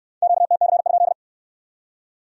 Morse code, heard as a keyed tone, about 57 wpm, 700 Hz. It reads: QTC?